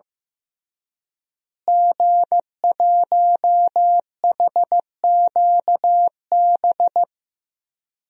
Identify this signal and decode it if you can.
Morse code, heard as a keyed tone, about 15 wpm, 700 Hz. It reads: G1HQB
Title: DXCC-100 Call Signs - 5 Characters - Repeated 3x - 15wpm